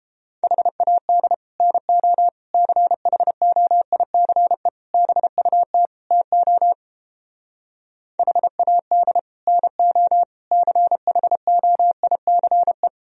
Morse code, heard as a keyed tone, about 29 wpm, 700 Hz. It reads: HAD NO CHOICE BUT TO HAD NO CHOICE